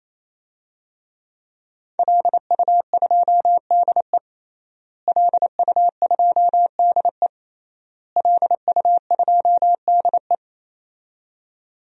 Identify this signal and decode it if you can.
Morse code, heard as a keyed tone, about 28 wpm, 700 Hz. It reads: LU2DE LU2DE LU2DE
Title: DXCC-100 Call Signs - 5 Characters - Repeated 3x - 28wpm